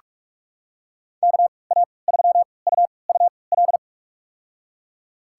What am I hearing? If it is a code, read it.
Morse code, heard as a keyed tone, about 46 wpm, spaced as for 26 wpm, 700 Hz. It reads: KA3UUL
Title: USA Call Signs - 6 Characters - 3x Character Spacing - 45wpm